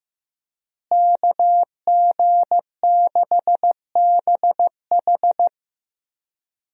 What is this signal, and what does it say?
Morse code, heard as a keyed tone, about 15 wpm, 700 Hz. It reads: KG6BH
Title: USA Call Signs - 5 Characters - Repeated 2x - 15wpm